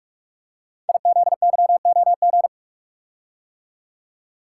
Morse code, heard as a keyed tone, about 44 wpm, 700 Hz. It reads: IZYOG
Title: Code Groups - 5 Letters - 45wpm